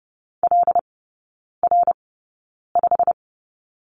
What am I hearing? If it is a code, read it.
Morse code, heard as a keyed tone, about 30 wpm, spaced as for 9 wpm, 700 Hz. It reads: LR5
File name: course / 30:10wpm / 087 Lesson 15e - Number 5 - Three Characters 30wpm